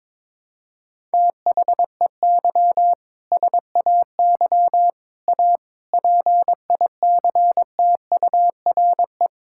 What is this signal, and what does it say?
Morse code, heard as a keyed tone, about 22 wpm, 700 Hz. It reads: THEY SAY A PICTURE
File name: Sentences with 7-Word Phrases - 22wpm